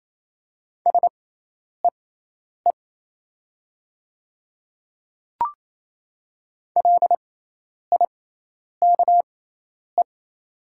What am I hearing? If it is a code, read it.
Morse code, heard as a keyed tone, about 28 wpm, spaced as for 10 wpm, 700 Hz. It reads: SEE LIKE
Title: 6x Character Spacing - Top 100 Words - 28wpm